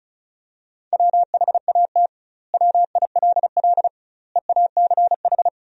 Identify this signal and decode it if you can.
Morse code, heard as a keyed tone, about 34 wpm, 700 Hz. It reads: WHAT WILL EACH
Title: 2x Word-Spacing - Sentences from Top 200 Words - 35wpm